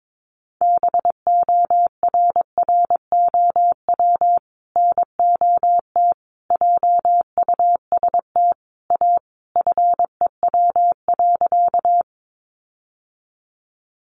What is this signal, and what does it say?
Morse code, heard as a keyed tone, about 22 wpm, 700 Hz. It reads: BORROW NOT JUST A FEW.
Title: Sentences from Top 1500 Words 22wpm